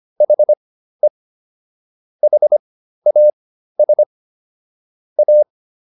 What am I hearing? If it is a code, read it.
Morse code, heard as a keyed tone, about 25 wpm, spaced as for 13 wpm, 600 Hz. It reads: HE HAS A